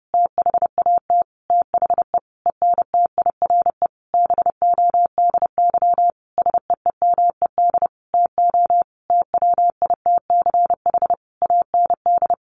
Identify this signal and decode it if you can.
Morse code, heard as a keyed tone, about 30 wpm, 700 Hz. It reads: THAT THE ENTIRE BODY SEEMED TO TWITCH AND
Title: A Sound Of Thunder 30wpm